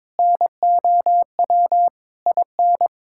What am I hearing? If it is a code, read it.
Morse code, heard as a keyed tone, about 22 wpm, 700 Hz. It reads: NOW IN
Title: Sentences from Top 1000 Words - Review Part 6 - 22wpm